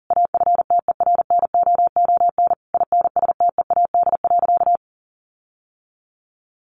Morse code, heard as a keyed tone, about 37 wpm, 700 Hz. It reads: AFTERNOON INSTEAD.
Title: Sentences from Top 1500 Words 40wpm